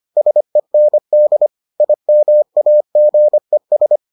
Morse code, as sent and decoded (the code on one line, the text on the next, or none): ... -. -.. / .. -- .- --. ...
SEND IMAGES